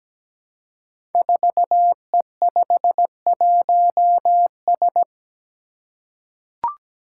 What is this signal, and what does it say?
Morse code, heard as a keyed tone, about 17 wpm, 700 Hz. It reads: 4E51S
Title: Five Challenging Letter-Number Combinations 17wpm